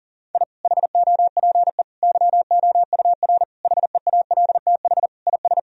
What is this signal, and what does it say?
Morse code, heard as a keyed tone, about 40 wpm, 700 Hz. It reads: I HOPE YOUR HEALTH IS
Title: Sentences from Top 1000 Words - Review Part 3 - 40wpm